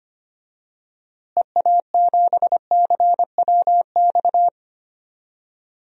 Morse code, heard as a keyed tone, about 25 wpm, 700 Hz. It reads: EA7CWX
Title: DXCC-100 Call Signs - 6 Characters - Repeated 2x - 25wpm